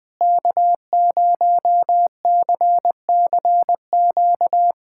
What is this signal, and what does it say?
Morse code, heard as a keyed tone, about 21 wpm, 700 Hz. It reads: K0CCQ
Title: USA Callsign DE Callsign - 5 Characters - 20wpm